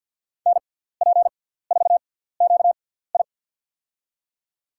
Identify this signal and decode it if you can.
Morse code, heard as a keyed tone, about 50 wpm, spaced as for 17 wpm, 700 Hz. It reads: NP4QI
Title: USA Call Signs - 5 Characters - 6x Character Spacing - 50wpm